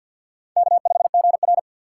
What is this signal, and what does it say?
Morse code, heard as a keyed tone, about 49 wpm, 700 Hz. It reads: K5GR
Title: USA Call Signs - 3x Character Spacing - 50wpm